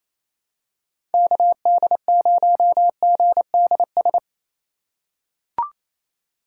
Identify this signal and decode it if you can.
Morse code, heard as a keyed tone, about 28 wpm, 700 Hz. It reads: KD0GDS
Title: USA Call Signs - 2x Character Spacing - 28wpm